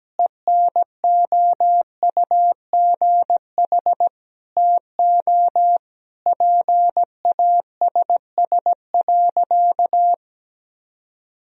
Morse code, heard as a keyed tone, about 17 wpm, 700 Hz. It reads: ENOUGH TO PASS.
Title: Sentences from Top 1000 Words - Review Part 2 - 17wpm